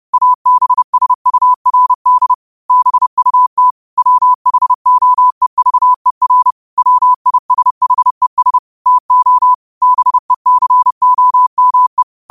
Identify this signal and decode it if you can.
Morse code, heard as a keyed tone, about 30 wpm, 1000 Hz. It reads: ABSURD BUT WHOEVER WISHES TO BECOME
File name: qotdfiles / 30wpm